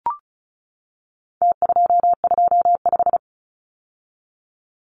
Morse code, heard as a keyed tone, about 34 wpm, 700 Hz. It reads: T225
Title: Numbers - Contest Style - 4-Digit 35wpm